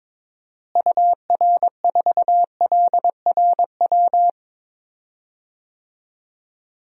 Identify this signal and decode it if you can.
Morse code, heard as a keyed tone, about 21 wpm, 700 Hz. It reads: UR4LRW
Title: DXCC-100 Call Signs - 6 Characters - Repeated 3x - 22wpm